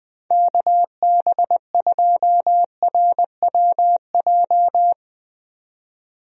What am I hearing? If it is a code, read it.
Morse code, heard as a keyed tone, about 20 wpm, 700 Hz. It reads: KB2RWJ
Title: USA Call Signs - 6 Characters - 20wpm